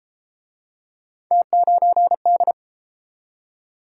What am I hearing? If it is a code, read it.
Morse code, heard as a keyed tone, about 33 wpm, 700 Hz. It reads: T9D